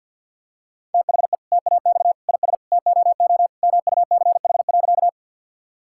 Morse code, heard as a keyed tone, about 52 wpm, 690 Hz. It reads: THE TAX IS TOO MUCH.
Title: Sentences from Top 1000 Words - Review Part 5 - 50wpm